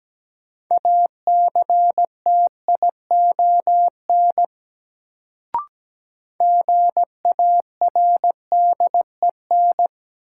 Morse code, heard as a keyed tone, about 17 wpm, 700 Hz. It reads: ACTION GARDEN
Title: Sentences from Top 700 Words - New Word Drill - 17wpm